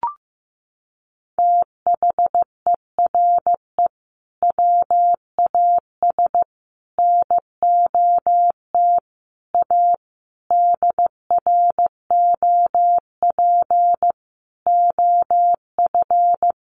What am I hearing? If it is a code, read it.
Morse code, heard as a keyed tone, about 15 wpm, 700 Hz. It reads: THERE WAS NOT A DROP OF
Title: Sentences from Top 1400 Words 15wpm